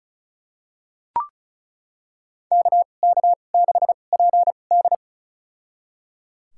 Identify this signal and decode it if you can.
Morse code, heard as a keyed tone, about 34 wpm, spaced as for 25 wpm, 700 Hz. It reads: KK6PD
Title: USA Call Signs - 2x Character Spacing - 35wpm